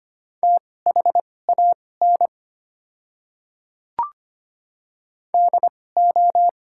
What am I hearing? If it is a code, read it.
Morse code, heard as a keyed tone, about 25 wpm, spaced as for 18 wpm, 690 Hz. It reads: THAN DO
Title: 2x Character Spacing - Top 100 Words - 25wpm